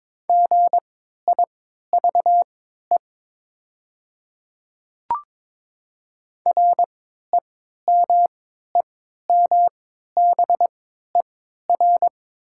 Morse code, heard as a keyed tone, about 22 wpm, spaced as for 12 wpm, 700 Hz. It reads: GIVE REMEMBER
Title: 3x Character Spacing - Top 500 Words - 22wpm